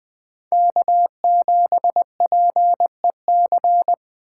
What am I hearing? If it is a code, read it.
Morse code, heard as a keyed tone, about 24 wpm, 700 Hz. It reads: K7PEC
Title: Speed Racing - USA 1x3 Call Signs 20wpm